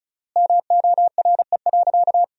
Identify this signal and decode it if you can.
Morse code, heard as a keyed tone, about 36 wpm, 700 Hz. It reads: MORE.
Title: 3x Word-Spacing - Sentences from Top 500 Words - 35wpm